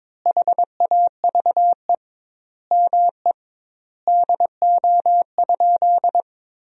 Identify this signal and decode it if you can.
Morse code, heard as a keyed tone, about 22 wpm, 700 Hz. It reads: HAVE ME DO?
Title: 2x Word-Spacing - Sentences from Top 200 Words - 22wpm